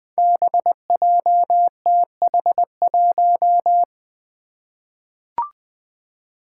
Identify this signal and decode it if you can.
Morse code, heard as a keyed tone, about 20 wpm, 700 Hz. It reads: BJTH1
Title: Five Challenging Letter-Number Combinations 20wpm